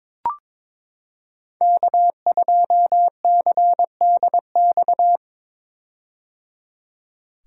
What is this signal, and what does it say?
Morse code, heard as a keyed tone, about 22 wpm, 700 Hz. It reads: K2CDX
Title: K1USN SST QSO Practice - 22wpm